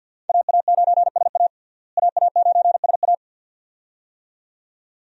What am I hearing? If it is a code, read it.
Morse code, heard as a keyed tone, about 53 wpm, 700 Hz. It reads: AA9SA AA9SA
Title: USA Call Signs - 5 Characters - Repeated 2x - 50wpm